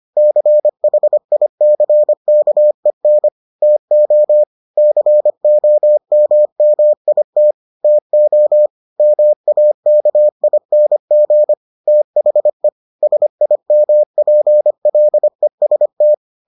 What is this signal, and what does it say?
Morse code, heard as a keyed tone, about 25 wpm, 600 Hz. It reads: CHICKEN TO COMMIT TO MAKING THE SIMPLEST